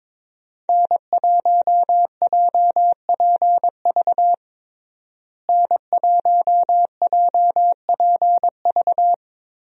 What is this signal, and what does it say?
Morse code, heard as a keyed tone, about 22 wpm, 700 Hz. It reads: N1JPV N1JPV
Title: USA Call Signs - 5 Characters - Repeated 3x - 22wpm